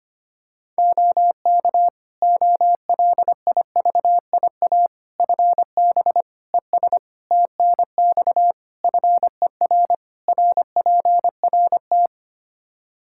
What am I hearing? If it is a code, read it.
Morse code, heard as a keyed tone, about 25 wpm, 700 Hz. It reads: OK OLIVIA FB ES TNX FER RPRT
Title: Practice QSO - Exchange 2 - Info - 25wpm